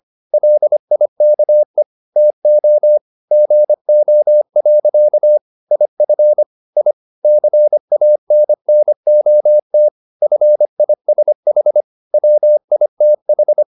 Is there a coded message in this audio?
Yes: Morse code from 0.3 to 13.6 s, heard as a keyed tone, about 25 wpm, 600 Hz. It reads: LIKE TO GO. IF I CANNOT FISH WITH